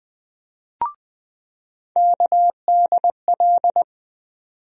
Morse code, heard as a keyed tone, about 20 wpm, 700 Hz. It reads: KDL
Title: USA Call Sign Postfixes - 3 Letters 20wpm